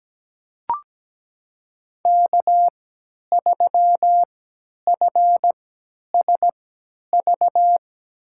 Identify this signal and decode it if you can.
Morse code, heard as a keyed tone, about 17 wpm, spaced as for 10 wpm, 700 Hz. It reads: K3FSV
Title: USA Call Signs - 5 Characters - 3x Character Spacing - 17wpm